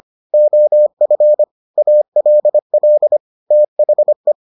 Morse code, heard as a keyed tone, about 25 wpm, 600 Hz. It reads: OF ALL THE